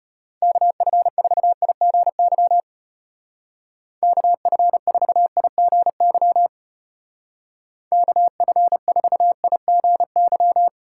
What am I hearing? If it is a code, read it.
Morse code, heard as a keyed tone, about 31 wpm, 700 Hz. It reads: KF4IGY KF4IGY KF4IGY